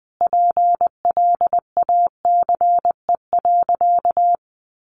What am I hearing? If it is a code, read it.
Morse code, heard as a keyed tone, about 20 wpm, 700 Hz. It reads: PLACE.
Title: Sentences from Top 1500 Words 20wpm